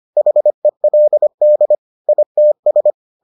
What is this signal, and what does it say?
Morse code, heard as a keyed tone, about 25 wpm, 600 Hz. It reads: HELD ITS